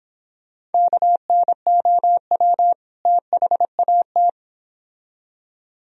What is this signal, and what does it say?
Morse code, heard as a keyed tone, about 30 wpm, 700 Hz. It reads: KNOW THAT